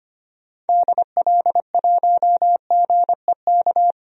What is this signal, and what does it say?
Morse code, heard as a keyed tone, about 25 wpm, 700 Hz. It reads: DL1GEK
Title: DXCC-100 Call Signs - 6 Characters - Repeated 2x - 25wpm